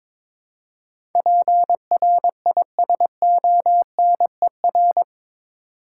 Morse code, heard as a keyed tone, about 22 wpm, 700 Hz. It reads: PRISONER
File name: practice / 22wpm / Eight Letter Words - 22wpm